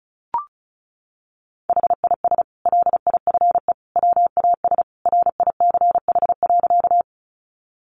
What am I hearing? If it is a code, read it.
Morse code, heard as a keyed tone, about 34 wpm, 700 Hz. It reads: HIS LIFE WAS RICH.
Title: Sentences from Top 1100 Words 35wpm